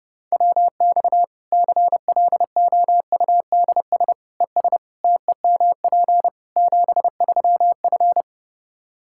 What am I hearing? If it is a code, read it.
Morse code, heard as a keyed tone, about 32 wpm, 700 Hz. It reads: WX CLOUDS ES TEMP 73F
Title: Practice QSO - Exchange 2 - Info - 30wpm